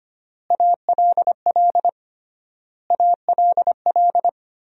Morse code, heard as a keyed tone, about 25 wpm, 700 Hz. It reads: ALL ALL
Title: 3x Word-Spacing - Sentences from Top 300 Words - 25wpm